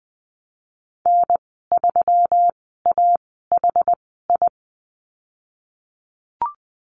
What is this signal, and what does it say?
Morse code, heard as a keyed tone, about 20 wpm, spaced as for 15 wpm, 700 Hz. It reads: N3AHI
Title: 193 Lesson 31g - Letter v - Call Signs 20wpm